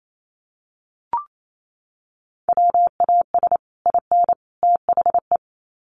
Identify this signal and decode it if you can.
Morse code, heard as a keyed tone, about 28 wpm, 700 Hz. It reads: WAS IN THE
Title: Sets of 3 Words 28wpm